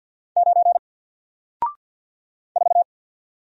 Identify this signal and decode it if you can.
Morse code, heard as a keyed tone, about 48 wpm, 700 Hz. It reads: 9 4
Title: Single Character (Letter:Number:Punctuation:Prosign) 50wpm